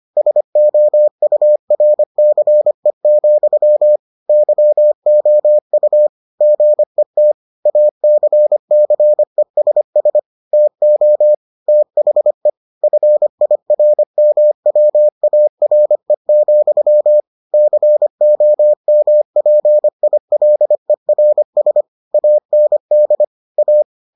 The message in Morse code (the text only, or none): SOURCE, YOU GET ACCESS TO THE FIRMWARE, COMPILERS AND A